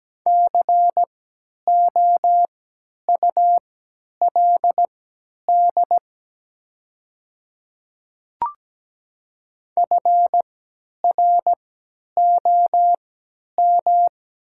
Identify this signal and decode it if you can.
Morse code, heard as a keyed tone, about 17 wpm, spaced as for 10 wpm, 700 Hz. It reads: COULD FROM
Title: 3x Character Spacing - Top 100 Words - 17wpm